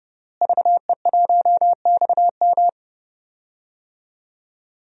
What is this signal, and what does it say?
Morse code, heard as a keyed tone, about 30 wpm, 700 Hz. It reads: VE1XM
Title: DXCC-100 Call Signs - 5 Characters - Repeated 2x - 30wpm